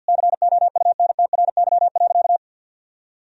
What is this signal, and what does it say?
Morse code, heard as a keyed tone, about 52 wpm, 700 Hz. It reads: COUNTRY.
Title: Sentences from Top 1000 Words - Review Part 3 - 50wpm